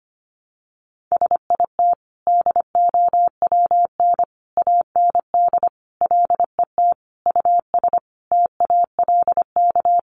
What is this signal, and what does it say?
Morse code, heard as a keyed tone, about 25 wpm, 700 Hz. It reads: SIT DOWN AND LET US TALK
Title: Sentences from Top 1300 Words 25wpm